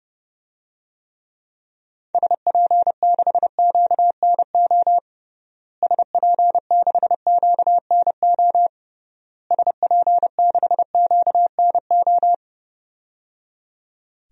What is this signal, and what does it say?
Morse code, heard as a keyed tone, about 30 wpm, 700 Hz. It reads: SP6QNO SP6QNO SP6QNO
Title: DXCC-100 Call Signs - 6 Characters - Repeated 3x - 30wpm